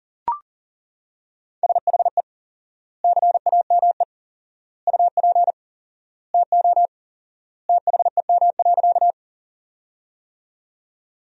Morse code, heard as a keyed tone, about 40 wpm, 700 Hz. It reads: SHE CAME UP TO THEM.
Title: 4x Word-Spacing - Sentences from Top 200 Words - 40wpm